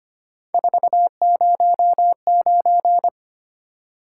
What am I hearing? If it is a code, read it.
Morse code, heard as a keyed tone, about 28 wpm, 700 Hz. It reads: 409
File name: Speed-Racing - Three Letter-Number Combinations - 25wpm